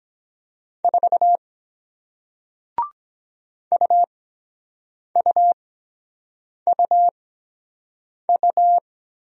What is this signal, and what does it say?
Morse code, heard as a keyed tone, about 23 wpm, spaced as for 7 wpm, 700 Hz. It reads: 4 UUUU